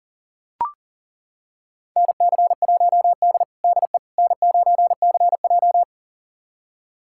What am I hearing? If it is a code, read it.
Morse code, heard as a keyed tone, about 40 wpm, 700 Hz. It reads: NC1D DE N9CJ
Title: USA Callsign DE Callsign - 4 Characters - 40wpm